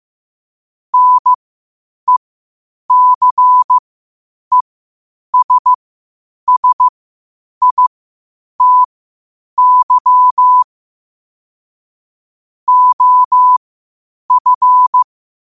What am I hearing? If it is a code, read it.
Morse code, heard as a keyed tone, about 15 wpm, spaced as for 8 wpm, 1000 Hz. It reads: NECESSITY OF
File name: qotdfiles / 8wpm